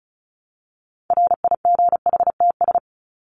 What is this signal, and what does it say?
Morse code, heard as a keyed tone, about 34 wpm, 700 Hz. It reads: RIGHTS
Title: ES:S Suffix - Top 100 Words with Base Words 35wpm